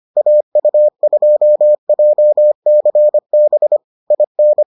Morse code, heard as a keyed tone, about 25 wpm, 600 Hz. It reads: AU2JCB IN